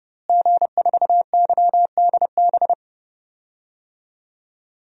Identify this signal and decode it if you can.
Morse code, heard as a keyed tone, about 30 wpm, 700 Hz. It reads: G4YDB
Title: DXCC-100 Call Signs - 5 Characters - Repeated 2x - 30wpm